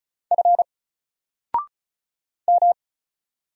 Morse code, heard as a keyed tone, about 34 wpm, 700 Hz. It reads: F M
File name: Single-Letter-Number-35wpm